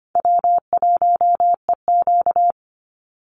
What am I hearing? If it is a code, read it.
Morse code, heard as a keyed tone, about 25 wpm, 700 Hz. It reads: W1EQ
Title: Easy Call Signs - 25wpm